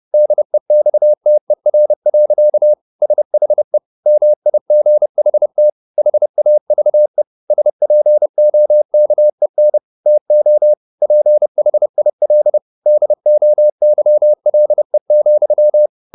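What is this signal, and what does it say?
Morse code, heard as a keyed tone, about 30 wpm, 600 Hz. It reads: DEXTER. SHE MIGHT HAVE SPOKEN TO PHIL DOYLE,
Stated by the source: Morse code at 30 wpm, the novel Never by Ken Follett in English, from CW med Gnister.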